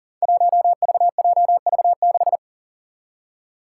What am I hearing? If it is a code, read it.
Morse code, heard as a keyed tone, about 42 wpm, 700 Hz. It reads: 1VJV6
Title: Five Challenging Letter-Number Combinations 40wpm